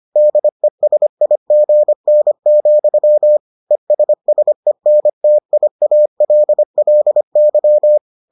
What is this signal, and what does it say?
Morse code, heard as a keyed tone, about 25 wpm, 600 Hz. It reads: DESIGN, ESSENTIALLY